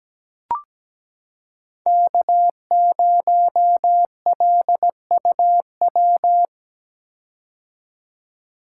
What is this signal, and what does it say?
Morse code, heard as a keyed tone, about 17 wpm, 700 Hz. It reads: K0LUW
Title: K1USN SST QSO Practice - 17wpm